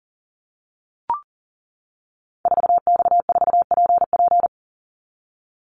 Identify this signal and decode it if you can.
Morse code, heard as a keyed tone, about 40 wpm, 700 Hz. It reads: 4X4PP
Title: Call Signs - 40wpm